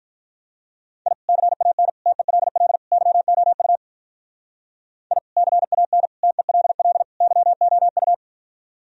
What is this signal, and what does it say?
Morse code, heard as a keyed tone, about 44 wpm, 700 Hz. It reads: I CAN TELL YOU I CAN TELL YOU